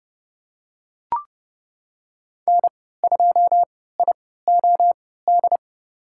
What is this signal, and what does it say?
Morse code, heard as a keyed tone, about 30 wpm, spaced as for 17 wpm, 700 Hz. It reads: N2IOD
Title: USA Call Signs - 5 Characters - 3x Character Spacing - 30wpm